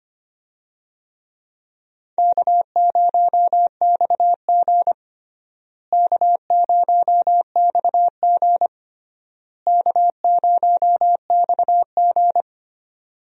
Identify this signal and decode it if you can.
Morse code, heard as a keyed tone, about 26 wpm, 700 Hz. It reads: K0XG K0XG K0XG
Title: USA Call Signs - 4 Characters - Repeated 3x - 25wpm